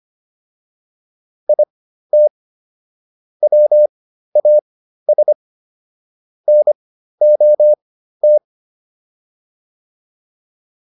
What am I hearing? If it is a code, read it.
Morse code, heard as a keyed tone, about 25 wpm, spaced as for 13 wpm, 600 Hz. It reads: IT WAS NOT